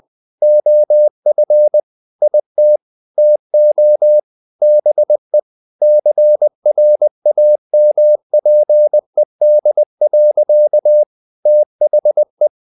Morse code, heard as a keyed tone, about 20 wpm, 600 Hz. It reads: OF IT TO BE CRAMPED. THE
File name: the old man and the sea / the old man and the sea part 1 20wpm